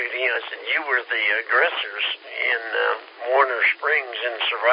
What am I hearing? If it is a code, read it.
Emergency Alert System (SAME) tones, heard as harsh buzzing bursts, with speech. It is heard from 1.0 to 4.1 s.